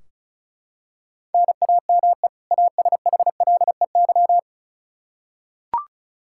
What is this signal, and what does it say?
Morse code, heard as a keyed tone, about 36 wpm, 700 Hz. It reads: NAME ASHLEY
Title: Practice QSO - Exchange 1 - Beginning - 35wpm